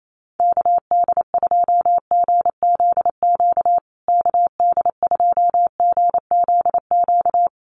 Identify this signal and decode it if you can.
Morse code, heard as a keyed tone, about 28 wpm, 700 Hz. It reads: KD2GZQ KD2GZQ
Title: USA Call Signs Repeated 3x 28wpm